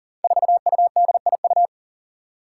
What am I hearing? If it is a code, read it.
Morse code, heard as a keyed tone, about 40 wpm, 700 Hz. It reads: 4UDIU
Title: Five Challenging Letter-Number Combinations 40wpm